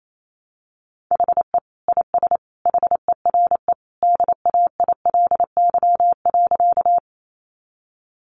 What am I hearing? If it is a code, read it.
Morse code, heard as a keyed tone, about 28 wpm, 700 Hz. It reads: HE IS HERE DAILY.
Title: Sentences from Top 1100 Words 28wpm